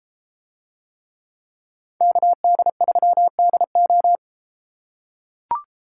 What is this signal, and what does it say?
Morse code, heard as a keyed tone, about 32 wpm, 700 Hz. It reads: KD3DO